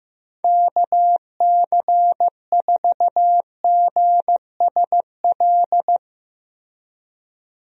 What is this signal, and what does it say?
Morse code, heard as a keyed tone, about 15 wpm, 700 Hz. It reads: KC4GSL
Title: USA Call Signs - 6 Characters - Repeated 2x - 15wpm